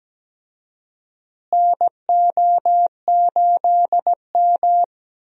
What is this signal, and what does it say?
Morse code, heard as a keyed tone, about 17 wpm, 700 Hz. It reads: NO8M
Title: USA Call Signs - 4 Characters - Repeated 2x - 17wpm